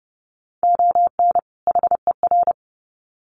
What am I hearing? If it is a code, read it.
Morse code, heard as a keyed tone, about 30 wpm, 700 Hz. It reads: ON HER
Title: Sets of 2 Words 30wpm